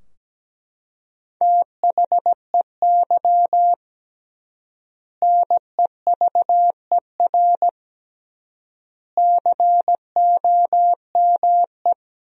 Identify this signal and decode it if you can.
Morse code, heard as a keyed tone, about 17 wpm, 700 Hz. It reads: THEY NEVER COME
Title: 3x Word-Spacing - Sentences from Top 300 Words - 17wpm